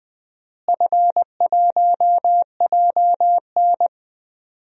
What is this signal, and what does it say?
Morse code, heard as a keyed tone, about 21 wpm, 700 Hz. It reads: F1JN
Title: DXCC-100 Call Signs - 4 Characters - Repeated 3x - 20wpm